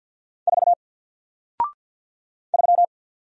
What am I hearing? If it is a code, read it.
Morse code, heard as a keyed tone, about 52 wpm, 700 Hz. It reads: 4 3
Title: Single Letter-Number Alt 50wpm